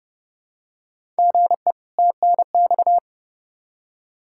Morse code, heard as a keyed tone, about 30 wpm, 700 Hz. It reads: GE TNX
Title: K1USN SST QSO Practice 30wpm